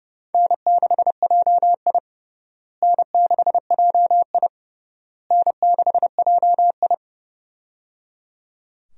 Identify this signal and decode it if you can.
Morse code, heard as a keyed tone, about 30 wpm, 700 Hz. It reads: N6JI N6JI N6JI
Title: USA Call Signs - 4 Characters - Repeated 3x - 30wpm